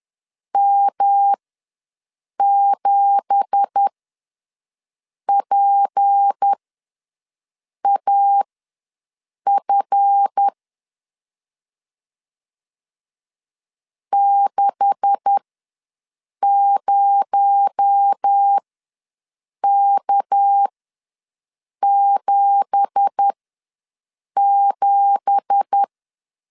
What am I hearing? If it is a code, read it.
Morse code, heard as a keyed tone, about 10 wpm, spaced as for 6 wpm, 800 Hz. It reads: M7PAF 60K77
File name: LEZIONI / 9 2867 ws 10wpm 4215